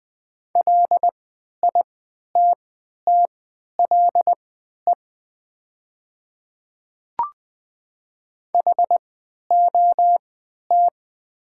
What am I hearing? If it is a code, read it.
Morse code, heard as a keyed tone, about 20 wpm, spaced as for 11 wpm, 700 Hz. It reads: LITTLE HOT
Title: 3x Character Spacing - Top 500 Words - 20wpm